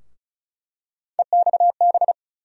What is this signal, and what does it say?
Morse code, heard as a keyed tone, about 36 wpm, 700 Hz. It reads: EXB
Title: USA Call Sign Postfixes - 3 Letters 35wpm